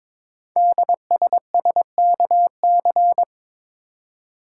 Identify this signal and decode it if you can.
Morse code, heard as a keyed tone, about 22 wpm, 700 Hz. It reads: DSSKC
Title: Code Groups - 5 Letters - 22wpm